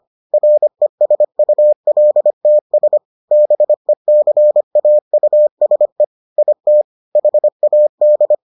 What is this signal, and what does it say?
Morse code, heard as a keyed tone, about 25 wpm, 600 Hz. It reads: RESULTS BECAUSE IT HAD